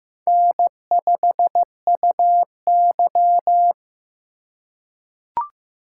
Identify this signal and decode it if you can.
Morse code, heard as a keyed tone, about 15 wpm, 700 Hz. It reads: N5UY
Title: USA Call Signs - 4 Characters - 2x Character Spacing - 15wpm